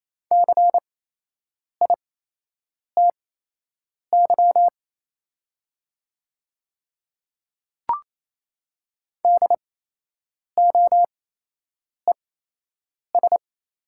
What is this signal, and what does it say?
Morse code, heard as a keyed tone, about 28 wpm, spaced as for 8 wpm, 700 Hz. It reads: CITY DOES
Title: 8x Character Spacing - Top 300 Words - 28wpm